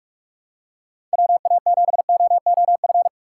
Morse code, heard as a keyed tone, about 47 wpm, 700 Hz. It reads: WA7OOF